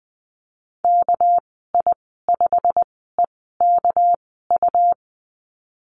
Morse code, heard as a keyed tone, about 20 wpm, spaced as for 15 wpm, 700 Hz. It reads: KI5EKU
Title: 213 Lesson 34g - Number 8 - Call Signs 20wpm